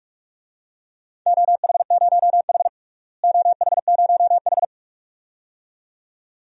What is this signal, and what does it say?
Morse code, heard as a keyed tone, about 46 wpm, 700 Hz. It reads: OH0H OH0H
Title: DXCC-100 Call Signs - 4 Characters - Repeated 2x - 45wpm